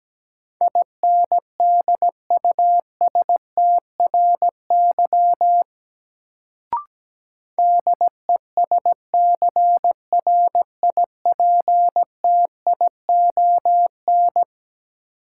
Morse code, heard as a keyed tone, about 17 wpm, 700 Hz. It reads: INDUSTRY DESCRIPTION
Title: Sentences from Top 1300 Words - New Word Drill Alt - 17wpm